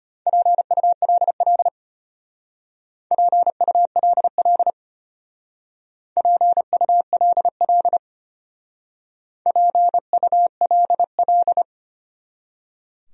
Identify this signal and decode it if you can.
Morse code, heard as a keyed tone, about 31 wpm, 700 Hz. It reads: PULL PULL PULL PULL